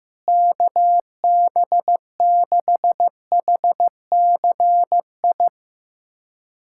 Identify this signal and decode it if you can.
Morse code, heard as a keyed tone, about 15 wpm, 700 Hz. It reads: KB6HCI
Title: USA Callsign DE Callsign 15wpm